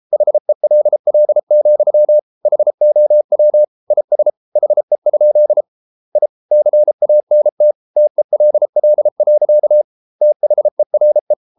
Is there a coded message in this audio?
Yes: Morse code from 0.1 to 11.3 s, heard as a keyed tone, about 33 wpm, 600 Hz. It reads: HELL, HOW IS HE? I CANT TELL. THERE